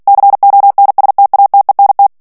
Morse code, heard as a keyed tone, about 48 wpm, 800 Hz. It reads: CONSTATENT
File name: CW / FICHIERS-MP3 / FRANCAIS TEXTE CLAIR 1 VITESSE 45 MOTS MINUTES